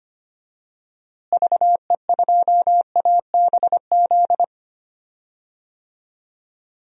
Morse code, heard as a keyed tone, about 25 wpm, 700 Hz. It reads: VE2ABZ
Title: DXCC-100 Call Signs - 6 Characters - Repeated 2x - 25wpm